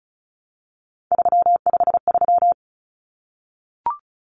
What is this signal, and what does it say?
Morse code, heard as a keyed tone, about 34 wpm, 700 Hz. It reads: 353
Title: Common Cut Numbers - 3-Digit 35wpm